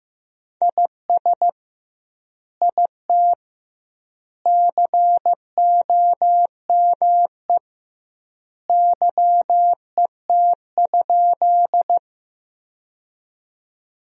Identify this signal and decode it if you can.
Morse code, heard as a keyed tone, about 15 wpm, 700 Hz. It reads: IS IT COME YET?
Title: 2x Word-Spacing - Sentences from Top 500 Words - 15wpm